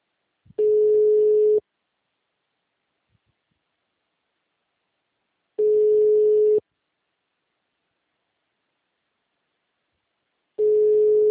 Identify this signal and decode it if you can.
Ringback tone (Europe, 425 Hz).